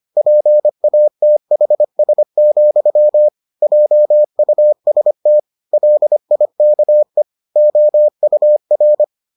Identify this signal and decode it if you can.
Morse code, heard as a keyed tone, about 25 wpm, 600 Hz. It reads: PATHS, JUST LIKE OUR